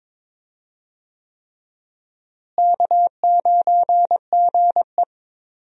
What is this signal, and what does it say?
Morse code, heard as a keyed tone, about 22 wpm, 700 Hz. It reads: K9GE
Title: USA Call Signs - 4 Characters - Repeated 3x - 22wpm